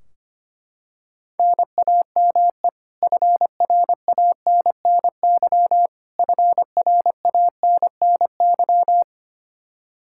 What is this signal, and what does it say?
Morse code, heard as a keyed tone, about 25 wpm, 700 Hz. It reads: NAME FRANNY FRANNY
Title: Practice QSO - Exchange 1 - Beginning - 25wpm